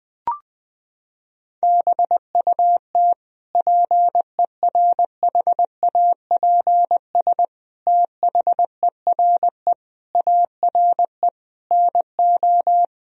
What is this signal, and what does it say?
Morse code, heard as a keyed tone, about 20 wpm, 700 Hz. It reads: BUT PERHAPS THERE ARE NO
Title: Sentences from Top 1000 Words - Review Part 3 - 20wpm